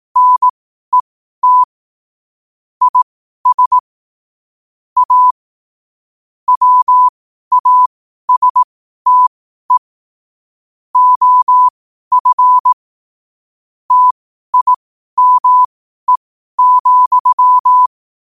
Morse code, heard as a keyed tone, about 18 wpm, spaced as for 12 wpm, 1000 Hz. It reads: NET IS A WASTE OF TIME,
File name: qotdfiles / 12wpm